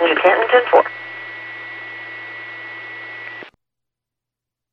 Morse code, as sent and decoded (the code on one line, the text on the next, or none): none